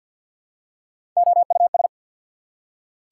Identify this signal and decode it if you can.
Morse code, heard as a keyed tone, about 48 wpm, 700 Hz. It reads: OUS
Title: Prefixes and Suffixes 50wpm